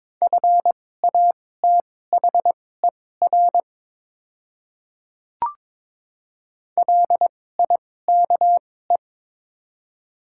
Morse code, heard as a keyed tone, about 22 wpm, spaced as for 16 wpm, 700 Hz. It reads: FATHER LIKE
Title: 2x Character Spacing - Top 300 Words - 22wpm